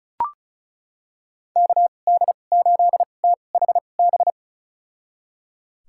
Morse code, heard as a keyed tone, about 36 wpm, spaced as for 25 wpm, 700 Hz. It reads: KD8THB
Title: USA Call Signs - 6 Characters - 2x Character Spacing - 35wpm